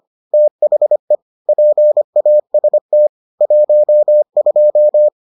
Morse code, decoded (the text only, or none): THE PAST 12